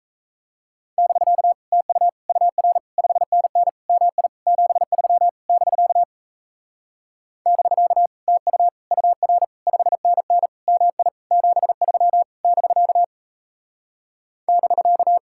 Morse code, heard as a keyed tone, about 37 wpm, 700 Hz. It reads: <BK> TU UR 5NN MI 73 <BK> <BK> TU UR 5NN MI 73 <BK> <BK>